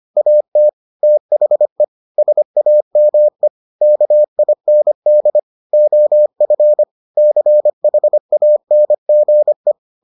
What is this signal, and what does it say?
Morse code, heard as a keyed tone, about 25 wpm, 600 Hz. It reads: AT THE SAME KIND OF CHANGE